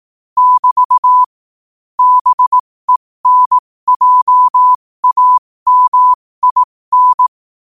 Morse code, heard as a keyed tone, about 18 wpm, spaced as for 15 wpm, 1000 Hz. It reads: <BT> BENJAMIN
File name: qotdfiles / 15wpm